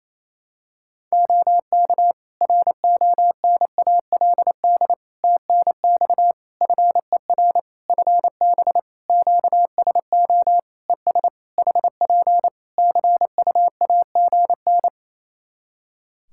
Morse code, heard as a keyed tone, about 28 wpm, 700 Hz. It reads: OK RONALD TNX FER FB QSO ES HP CUAGN
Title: Practice QSO - Exchange 4 - Ending - 28wpm